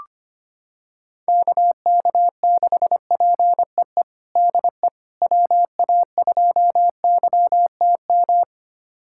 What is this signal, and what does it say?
Morse code, heard as a keyed tone, about 25 wpm, 700 Hz. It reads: KK6PEE DE WA2YTM
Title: USA Callsign DE Callsign - 6 Characters - 25wpm